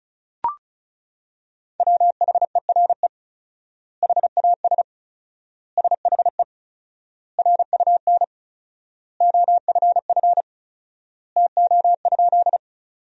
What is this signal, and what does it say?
Morse code, heard as a keyed tone, about 34 wpm, 700 Hz. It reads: WHERE HAS SHE RUN OFF TO?
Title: 4x Word-Spacing - Sentences from Top 300 Words - 35wpm